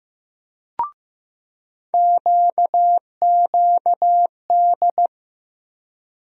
Morse code, heard as a keyed tone, about 15 wpm, 700 Hz. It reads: QQD
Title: USA Call Sign Postfixes - 3 Letters 15wpm